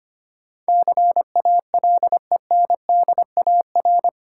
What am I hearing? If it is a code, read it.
Morse code, heard as a keyed tone, about 25 wpm, 700 Hz. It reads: CALENDAR
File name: practice / 25wpm / Easy Words with Hints - 25wpm